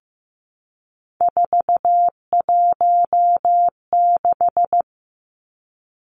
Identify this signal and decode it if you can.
Morse code, heard as a keyed tone, about 15 wpm, 700 Hz. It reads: 416
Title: Numbers - 3-Digit 15wpm